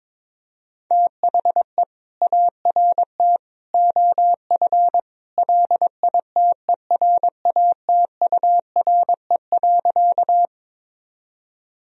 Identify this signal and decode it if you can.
Morse code, heard as a keyed tone, about 22 wpm, 700 Hz. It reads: THE ART OF LITERATURE.
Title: Sentences from Top 1000 Words - Review Part 3 - 22wpm